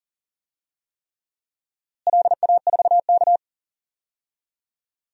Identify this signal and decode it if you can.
Morse code, heard as a keyed tone, about 40 wpm, 700 Hz. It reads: LA4K
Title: DXCC-100 Call Signs - 4 Characters - Repeated 3x - 40wpm